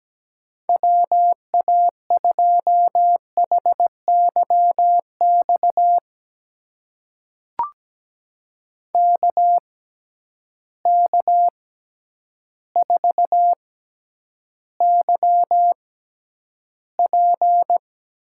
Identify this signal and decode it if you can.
Morse code, heard as a keyed tone, about 17 wpm, spaced as for 9 wpm, 700 Hz. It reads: WA2HYX KK4YP
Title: USA Call Signs - 6 Characters - 6x Character Spacing - 17wpm